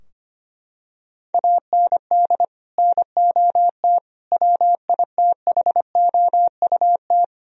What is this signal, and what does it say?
Morse code, heard as a keyed tone, about 25 wpm, 700 Hz. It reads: AND NOT WITHOUT
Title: Sentences from Top 1000 Words - Review Part 3 - 25wpm